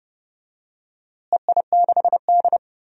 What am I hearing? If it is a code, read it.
Morse code, heard as a keyed tone, about 30 wpm, 700 Hz. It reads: EI6D
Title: DXCC-100 Call Signs - 4 Characters - Repeated 2x - 30wpm